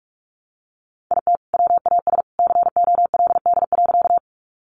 Morse code, heard as a keyed tone, about 44 wpm, 700 Hz. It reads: IT WAS COLD.
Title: Sentences from Top 400 Words 45wpm